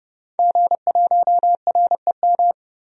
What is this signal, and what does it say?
Morse code, heard as a keyed tone, about 30 wpm, 700 Hz. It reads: G1REM
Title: DXCC-100 Call Signs - 5 Characters - 30wpm